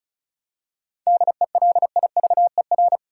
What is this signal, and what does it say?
Morse code, heard as a keyed tone, about 34 wpm, 700 Hz. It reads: DELIVER
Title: Seven Letter Words - 35wpm